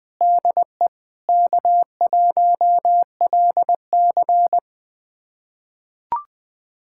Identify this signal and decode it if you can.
Morse code, heard as a keyed tone, about 20 wpm, 700 Hz. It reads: DE K1LC
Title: USA Callsign DE Callsign - 4 Characters - 20wpm